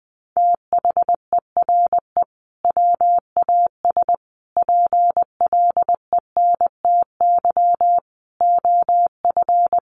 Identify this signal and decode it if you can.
Morse code, heard as a keyed tone, about 20 wpm, 700 Hz. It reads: THERE WAS PLENTY OF
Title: Sentences from Top 1300 Words 20wpm